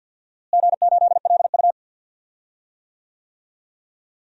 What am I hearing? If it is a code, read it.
Morse code, heard as a keyed tone, about 51 wpm, 700 Hz. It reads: G8LU
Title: DXCC-100 Call Signs - 4 Characters - Repeated 2x - 50wpm